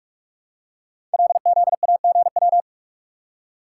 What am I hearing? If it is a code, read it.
Morse code, heard as a keyed tone, about 44 wpm, 700 Hz. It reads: LZAGW
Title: Code Groups - 5 Letter:Numbers - 45wpm